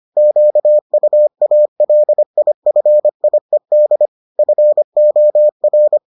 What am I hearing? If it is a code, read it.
Morse code, heard as a keyed tone, about 25 wpm, 600 Hz. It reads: QUALIFIED FOR